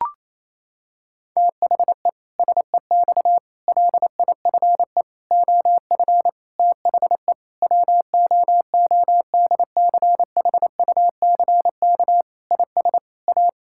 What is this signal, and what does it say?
Morse code, heard as a keyed tone, about 28 wpm, 700 Hz. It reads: THE SEX LIFE OF THE WOODCHUCK IS A
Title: Pangram Challenge v2 - 28wpm